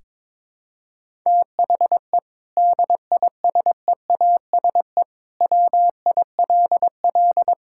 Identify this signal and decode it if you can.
Morse code, heard as a keyed tone, about 22 wpm, 700 Hz. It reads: THE DISEASE WILL
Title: Sentences from Top 1000 Words - Review Part 6 - 22wpm